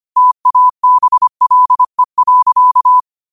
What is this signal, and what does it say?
Morse code, heard as a keyed tone, about 25 wpm, 1000 Hz. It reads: TABLE.
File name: qotdfiles / 25wpm